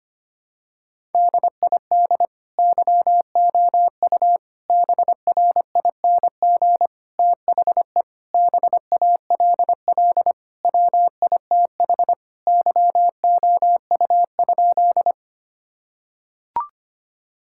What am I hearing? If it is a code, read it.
Morse code, heard as a keyed tone, about 25 wpm, 700 Hz. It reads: DID YOU BRING THE BALL WITH YOU?
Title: Sentences from Top 1000 Words - Review Part 6 - 25wpm